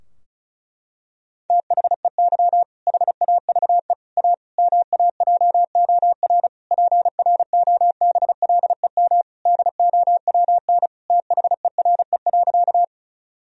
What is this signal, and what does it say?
Morse code, heard as a keyed tone, about 36 wpm, 700 Hz. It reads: THEY HAVE A MAJOR PROBLEM DOWN THERE.
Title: Sentences from Top 1000 Words - Review Part 4 - 35wpm